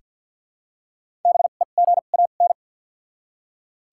Morse code, heard as a keyed tone, about 47 wpm, 700 Hz. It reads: B E G A N
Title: 2x Character Spacing - Top 500 Words - 50wpm